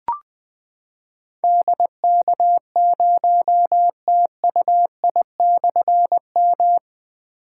Morse code, heard as a keyed tone, about 20 wpm, 700 Hz. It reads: DK0TUI/M
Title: Difficult Call Signs 20wpm